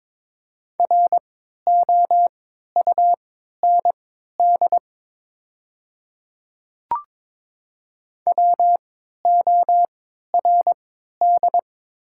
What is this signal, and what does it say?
Morse code, heard as a keyed tone, about 22 wpm, spaced as for 12 wpm, 700 Hz. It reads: ROUND WORD
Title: 3x Character Spacing - Top 200 Words - 22wpm